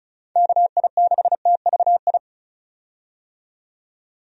Morse code, heard as a keyed tone, about 34 wpm, 700 Hz. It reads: KI6TVI
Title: USA Call Signs - 6 Characters - 35wpm